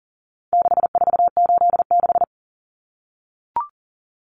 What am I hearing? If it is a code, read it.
Morse code, heard as a keyed tone, about 37 wpm, 700 Hz. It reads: 6486
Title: Numbers - 4-Digit 40wpm